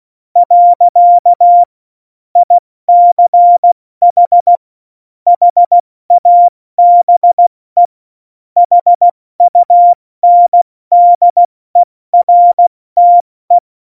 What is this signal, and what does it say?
Morse code, heard as a keyed tone, about 16 wpm, 700 Hz. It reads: . ICH HABE HUNDERTE
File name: Morsekurs / Witze / Witz 06 March 2018, 70 F80